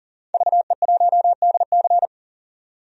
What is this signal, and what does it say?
Morse code, heard as a keyed tone, about 40 wpm, 700 Hz. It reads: VE1DC
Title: DXCC-100 Call Signs - 5 Characters - 40wpm